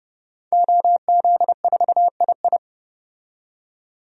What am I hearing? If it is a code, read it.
Morse code, heard as a keyed tone, about 32 wpm, 700 Hz. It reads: OZ4II
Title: DXCC-100 Call Signs - 5 Characters - 30wpm